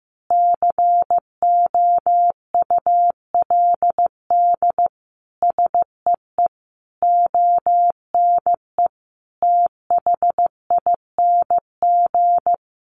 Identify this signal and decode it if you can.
Morse code, heard as a keyed tone, about 15 wpm, 700 Hz. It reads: COULD SEE ONE THING
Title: Sentences from Top 1100 Words 15wpm